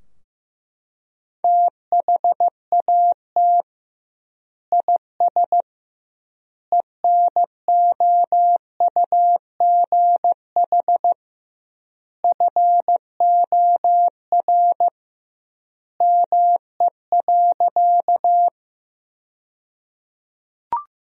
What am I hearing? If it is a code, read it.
Morse code, heard as a keyed tone, about 15 wpm, 700 Hz. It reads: THAT IS ENOUGH FOR ME.
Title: 2x Word-Spacing - Sentences from Top 400 Words - 15wpm